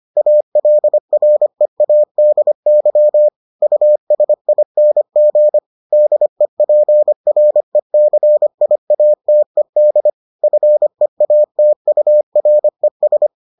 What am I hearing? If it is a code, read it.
Morse code, heard as a keyed tone, about 25 wpm, 600 Hz. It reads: ALREADY USING DEPRECIATED FEATURES